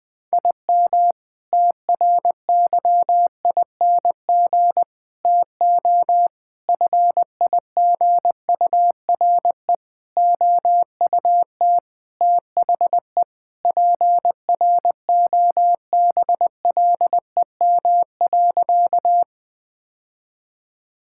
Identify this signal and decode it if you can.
Morse code, heard as a keyed tone, about 20 wpm, 700 Hz. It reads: IM TRYING TO FIGURE OUT THE PROBLEM.
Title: Sentences with 4-Word Phrases - 20wpm